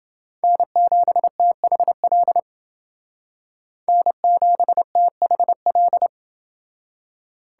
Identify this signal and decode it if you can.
Morse code, heard as a keyed tone, about 28 wpm, 700 Hz. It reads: N7THL N7THL